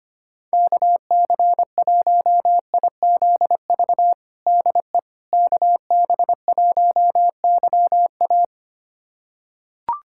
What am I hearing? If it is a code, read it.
Morse code, heard as a keyed tone, about 25 wpm, 700 Hz. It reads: KC1IZV DE KB1YA
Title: USA Callsign DE Callsign 25wpm